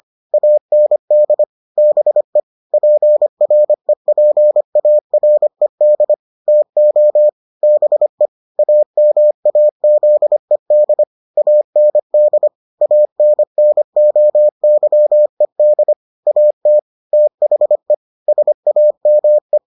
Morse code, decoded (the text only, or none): AND BE PREPARED TO BE AMAZED AND ANNOYED AT THE SAME